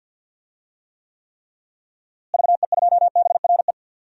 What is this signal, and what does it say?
Morse code, heard as a keyed tone, about 52 wpm, 700 Hz. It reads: VE2BRE
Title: DXCC-100 Call Signs - 6 Characters - Repeated 3x - 50wpm